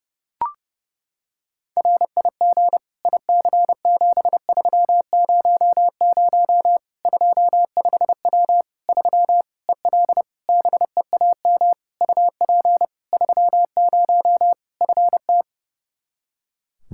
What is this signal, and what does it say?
Morse code, heard as a keyed tone, about 32 wpm, 700 Hz. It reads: RIG IC7300 25W 3 EL BEAM UP 30 FT
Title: Practice QSO - Exchange 2 - Info - 30wpm